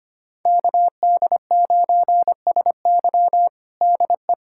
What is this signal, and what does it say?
Morse code, heard as a keyed tone, about 25 wpm, 700 Hz. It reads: KD9SY DE
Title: USA Callsign DE Callsign 25wpm